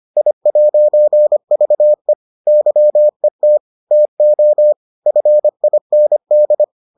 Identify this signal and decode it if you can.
Morse code, heard as a keyed tone, about 25 wpm, 600 Hz. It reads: I'VE YET TO FIND